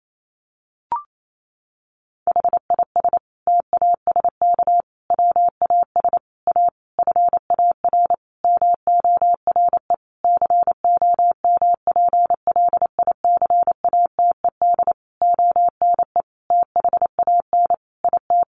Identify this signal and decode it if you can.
Morse code, heard as a keyed tone, about 28 wpm, 700 Hz. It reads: HIS TASK WAS A FAR MORE COMPLICATED ONE THAN IT
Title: Sentences from Top 1200 Words 28wpm